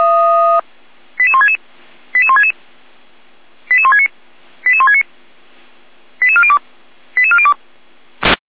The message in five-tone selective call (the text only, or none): 88166 88166 88169 88169 88402 88402